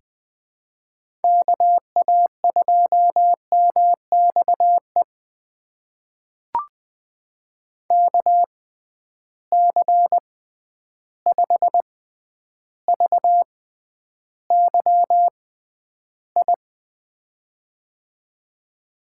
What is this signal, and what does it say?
Morse code, heard as a keyed tone, about 20 wpm, spaced as for 10 wpm, 700 Hz. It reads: KA2MXE KC5VYI
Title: USA Call Signs - 6 Characters - 6x Character Spacing - 20wpm